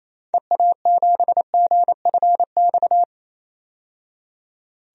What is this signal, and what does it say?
Morse code, heard as a keyed tone, about 28 wpm, 700 Hz. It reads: EA7GFX
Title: DXCC-100 Call Signs - 6 Characters - 28wpm